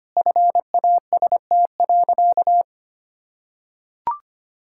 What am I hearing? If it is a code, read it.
Morse code, heard as a keyed tone, about 25 wpm, 700 Hz. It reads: FAST.
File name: practice / 25wpm / Sentences with 3-Word Phrases - 25wpm